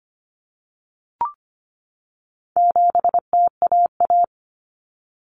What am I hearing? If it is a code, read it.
Morse code, heard as a keyed tone, about 25 wpm, 700 Hz. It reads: 7TAA
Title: Common Cut Numbers - 4-Digit 25wpm